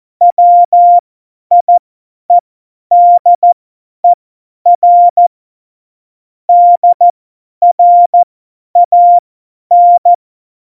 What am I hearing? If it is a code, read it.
Morse code, heard as a keyed tone, about 14 wpm, spaced as for 10 wpm, 700 Hz. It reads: WIEDER DRAN
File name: Morsekurs / Witze / Witz 27. November 2018, 50 F70